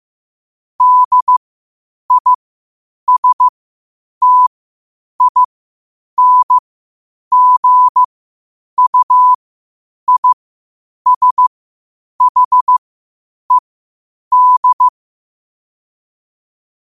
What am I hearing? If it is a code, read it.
Morse code, heard as a keyed tone, about 15 wpm, spaced as for 8 wpm, 1000 Hz. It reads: DISTINGUISHED
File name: qotdfiles / 8wpm